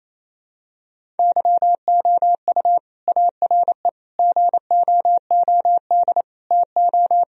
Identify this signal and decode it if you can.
Morse code, heard as a keyed tone, about 27 wpm, 700 Hz. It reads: YOU ARE GOOD TO
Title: Sentences from Top 1000 Words - Review Part 1 - 28wpm